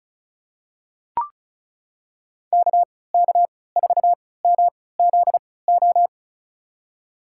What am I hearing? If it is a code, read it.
Morse code, heard as a keyed tone, about 34 wpm, spaced as for 20 wpm, 700 Hz. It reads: KK4MZO
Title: USA Call Signs - 3x Character Spacing - 35wpm